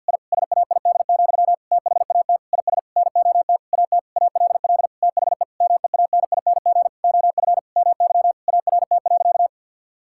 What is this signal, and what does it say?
Morse code, heard as a keyed tone, about 50 wpm, 700 Hz. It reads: I SAID, THAT IS NOT AT ALL THE MEANING OF MY ART.
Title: Sentences from Top 1000 Words - Review Part 6 - 50wpm